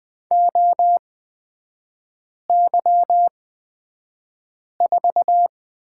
Morse code, heard as a keyed tone, about 20 wpm, spaced as for 5 wpm, 700 Hz. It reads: OY4
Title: 250 Lesson 40e - BK - Three Characters 20wpm